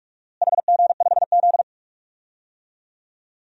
Morse code, heard as a keyed tone, about 41 wpm, 700 Hz. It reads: HG5Z
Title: DXCC-100 Call Signs - 4 Characters - Repeated 2x - 45wpm